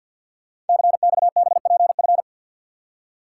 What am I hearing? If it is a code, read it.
Morse code, heard as a keyed tone, about 50 wpm, 700 Hz. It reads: CXBPF
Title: Code Groups - 5 Letters - 50wpm